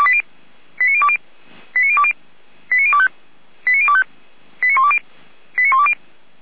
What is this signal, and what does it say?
Five-tone selective call (ZVEI), heard as quick runs of five tones, tones of about 70 ms. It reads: 89020 89020 89025 89025 89120 89120
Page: /455/SDR/ZVEI/89270/